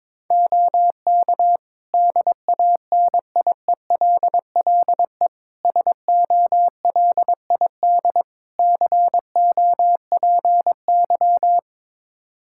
Morse code, heard as a keyed tone, about 22 wpm, 700 Hz. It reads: OK DANIELLE SOLID COPY
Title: Practice QSO - Exchange 3 - More Info - 22wpm